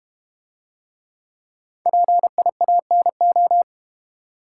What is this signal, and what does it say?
Morse code, heard as a keyed tone, about 32 wpm, 700 Hz. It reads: PIANO